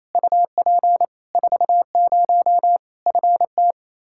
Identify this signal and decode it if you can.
Morse code, heard as a keyed tone, about 28 wpm, 700 Hz. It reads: UP 40 FT
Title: Practice QSO - Exchange 2 - Info - 28wpm